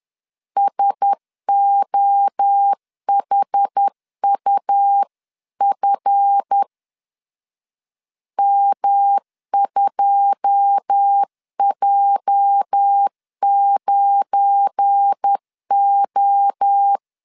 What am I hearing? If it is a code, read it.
Morse code, heard as a keyed tone, about 10 wpm, 800 Hz. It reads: SOHUF M2J9O